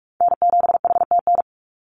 Morse code, heard as a keyed tone, about 46 wpm, 700 Hz. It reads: N7HTN
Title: USA Call Signs 45wpm